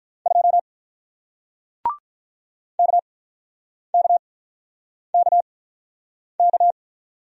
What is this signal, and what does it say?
Morse code, heard as a keyed tone, about 44 wpm, spaced as for 9 wpm, 700 Hz. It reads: 2 KKKK